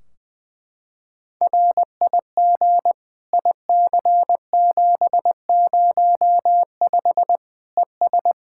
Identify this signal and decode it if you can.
Morse code, heard as a keyed tone, about 20 wpm, 700 Hz. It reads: RIG IC705 ES
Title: Practice QSO - Exchange 2 - Info - 20wpm